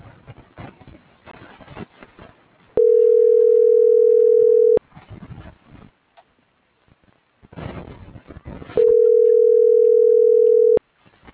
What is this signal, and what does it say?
Ringback tone (North America), 440 + 480 Hz.